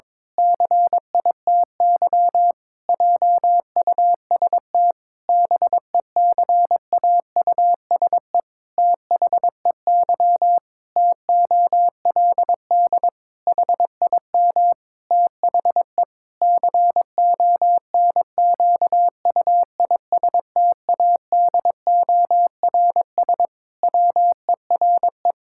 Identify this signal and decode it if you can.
Morse code, heard as a keyed tone, about 22 wpm, 700 Hz. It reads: CITY JUST BECAUSE THEY TOLD HIM THE CONQUISTADORS WERE
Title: Pangram Challenge v2 - 22wpm